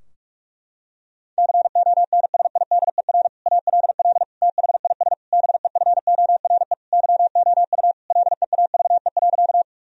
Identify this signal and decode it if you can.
Morse code, heard as a keyed tone, about 41 wpm, 700 Hz. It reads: CONSIDER ALL THIS BEFORE YOU LEAVE.
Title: Sentences from Top 1000 Words - Review Part 4 - 45wpm